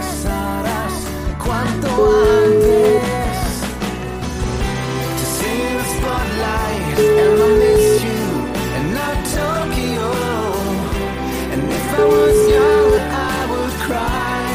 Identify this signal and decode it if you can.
Ringback tone (Europe, 425 Hz), with noise.